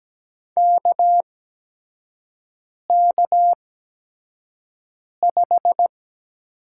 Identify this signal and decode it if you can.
Morse code, heard as a keyed tone, about 17 wpm, spaced as for 5 wpm, 700 Hz. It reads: KK5
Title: USA Call Signs - 5 Characters - 8x Character Spacing - 17wpm